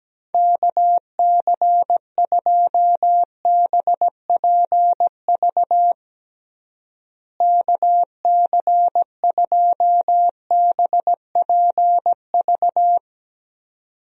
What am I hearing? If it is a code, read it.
Morse code, heard as a keyed tone, about 17 wpm, 700 Hz. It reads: KC2BPV KC2BPV
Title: USA Call Signs - 6 Characters - Repeated 2x - 17wpm